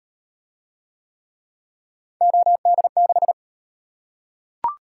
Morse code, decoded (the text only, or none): OD6